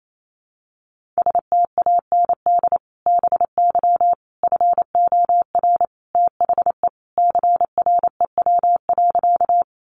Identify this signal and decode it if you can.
Morse code, heard as a keyed tone, about 28 wpm, 700 Hz. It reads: STAND BY FOR THE CREW.
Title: Sentences from Top 1500 Words 28wpm